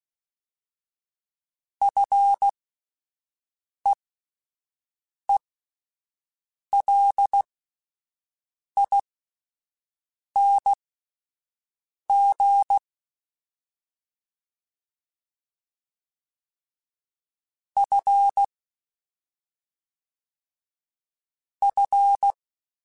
Morse code, heard as a keyed tone, about 16 wpm, spaced as for 5 wpm, 800 Hz. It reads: FEELING F F
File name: Fs / Single F mp3s / Single F Prefixes 3x2 Twice @5